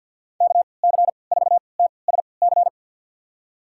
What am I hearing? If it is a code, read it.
Morse code, heard as a keyed tone, about 48 wpm, spaced as for 28 wpm, 700 Hz. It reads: KC4TSC
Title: USA Call Signs - 6 Characters - 3x Character Spacing - 50wpm